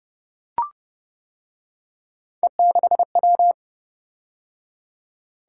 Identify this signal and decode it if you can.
Morse code, heard as a keyed tone, about 28 wpm, 700 Hz. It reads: E6W